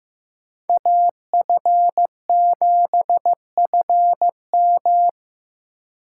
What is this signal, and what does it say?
Morse code, heard as a keyed tone, about 15 wpm, 700 Hz. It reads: AF7FM
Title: USA Call Signs - 5 Characters - 3x Character Spacing - 15wpm